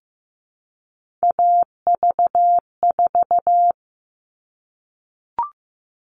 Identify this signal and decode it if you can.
Morse code, heard as a keyed tone, about 15 wpm, 700 Hz. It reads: AV4
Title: All Cut Numbers - 3-Digit 15wpm